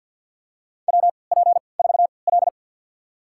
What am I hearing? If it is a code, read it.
Morse code, heard as a keyed tone, about 52 wpm, spaced as for 29 wpm, 700 Hz. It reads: WP4L
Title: USA Call Signs - 3x Character Spacing - 50wpm